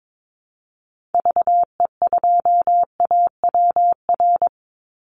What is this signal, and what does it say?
Morse code, heard as a keyed tone, about 22 wpm, 700 Hz. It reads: VE2AWR
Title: Call Signs Repeated 3x - 22wpm